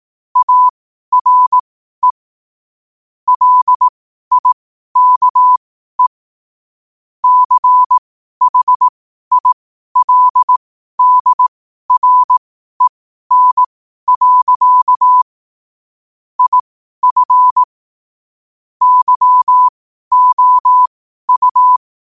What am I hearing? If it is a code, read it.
Morse code, heard as a keyed tone, about 18 wpm, spaced as for 12 wpm, 1000 Hz. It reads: ARE LIKE CHILDREN. IF YOU
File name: qotdfiles / 12wpm